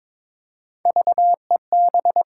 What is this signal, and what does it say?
Morse code, heard as a keyed tone, about 22 wpm, 700 Hz. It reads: VEB
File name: practice / 22wpm / Three Letter-Number Combinations - 22wpm